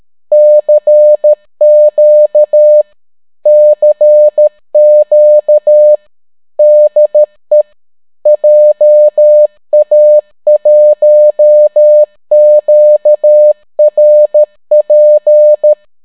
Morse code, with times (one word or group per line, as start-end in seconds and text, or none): 0.3-2.8 CQ
3.5-6.0 CQ
6.6-7.6 DE
8.3-15.7 JA1QRP